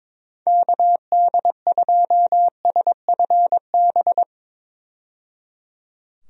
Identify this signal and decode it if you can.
Morse code, heard as a keyed tone, about 21 wpm, 700 Hz. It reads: KD2SFB
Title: USA Call Signs - 6 Characters - 22wpm